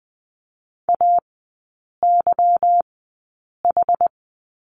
Morse code, heard as a keyed tone, about 20 wpm, spaced as for 8 wpm, 700 Hz. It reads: AYH
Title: 152 Lesson 25e - Letter p - Three Characters 20wpm